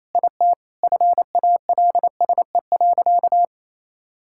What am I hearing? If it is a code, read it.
Morse code, heard as a keyed tone, about 28 wpm, 700 Hz. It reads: IT FALSE.
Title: Sentences from Top 1600 Words 28wpm